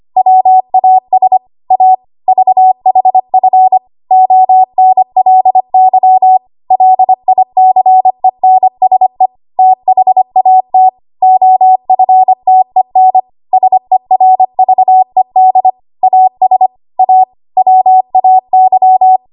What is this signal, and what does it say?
Morse code, heard as a keyed tone, about 25 wpm, 750 Hz. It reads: WAS A VHF ONLY LICENSE THAT OFTEN SERVED AS A WAY
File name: Archive / 25 WPM / 25WPM